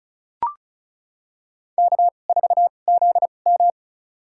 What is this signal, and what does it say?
Morse code, heard as a keyed tone, about 34 wpm, spaced as for 25 wpm, 700 Hz. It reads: K4ZM
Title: USA Call Signs - 2x Character Spacing - 35wpm